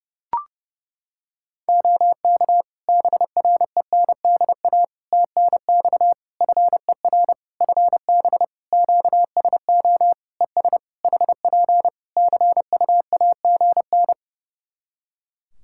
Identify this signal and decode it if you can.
Morse code, heard as a keyed tone, about 30 wpm, 700 Hz. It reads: OK BRENDA TNX FER FB QSO ES HP CUAGN
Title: Practice QSO - Exchange 4 - Ending - 30wpm